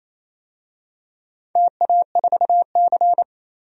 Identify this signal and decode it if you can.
Morse code, heard as a keyed tone, about 28 wpm, 700 Hz. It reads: TA4C
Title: DXCC-100 Call Signs - 4 Characters - 28wpm